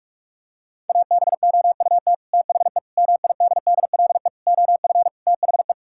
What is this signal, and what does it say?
Morse code, heard as a keyed tone, about 46 wpm, 700 Hz. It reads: ABOUT THE MIDDLE OF THE
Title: Sentences from Top 1000 Words - Review Part 6 - 45wpm